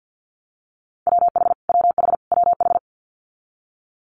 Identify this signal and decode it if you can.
Morse code, heard as a keyed tone, about 50 wpm, 700 Hz. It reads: WH WH WH
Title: Multi-letter Phonemes Repeated 3x 50wpm